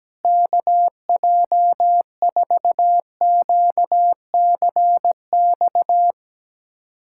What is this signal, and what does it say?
Morse code, heard as a keyed tone, about 17 wpm, 700 Hz. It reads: KJ4QCX
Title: USA Callsign DE Callsign - 6 Characters - 17wpm